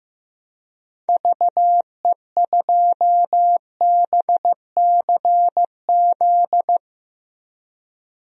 Morse code, heard as a keyed tone, about 15 wpm, 700 Hz. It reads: VE2BCZ
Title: DXCC-100 Call Signs - 6 Characters - Repeated 2x - 15wpm